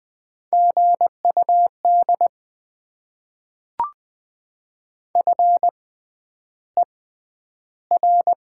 Morse code, heard as a keyed tone, about 20 wpm, 700 Hz. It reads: G U D F E R
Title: Top 20 QSO Elements - 6x Character Spacing - 20wpm